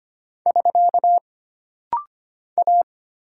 Morse code, heard as a keyed tone, about 25 wpm, 700 Hz. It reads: <SK> A
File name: Single Character (Letter:Number:Punctuation:Prosign) - Rapid-Fire - 25wpm